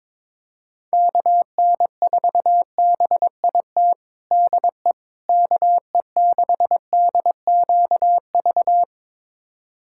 Morse code, heard as a keyed tone, about 22 wpm, 700 Hz. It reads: KN4BIT DE KE6DQV
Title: USA Callsign DE Callsign - 6 Characters - 22wpm